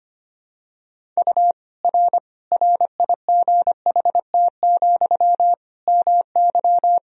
Morse code, heard as a keyed tone, about 25 wpm, 700 Hz. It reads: U R RIGHT, MY
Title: Sentences from Top 400 Words and Abbreviations 25wpm